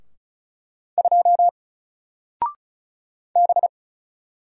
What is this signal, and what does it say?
Morse code, heard as a keyed tone, about 33 wpm, 700 Hz. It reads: 2 B
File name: Single-Letter-Number-35wpm